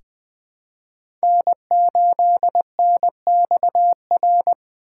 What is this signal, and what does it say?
Morse code, heard as a keyed tone, about 20 wpm, 700 Hz. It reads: N8NXR
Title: USA Call Signs - 5 Characters - 2x Character Spacing - 20wpm